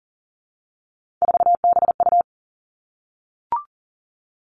Morse code, heard as a keyed tone, about 42 wpm, 700 Hz. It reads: 4BU
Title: Three Challenging Letter-Number Combinations 40wpm